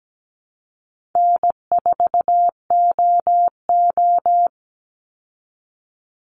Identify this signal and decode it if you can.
Morse code, heard as a keyed tone, about 17 wpm, 700 Hz. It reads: N4OO
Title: Call Signs Repeated 2x - 17wpm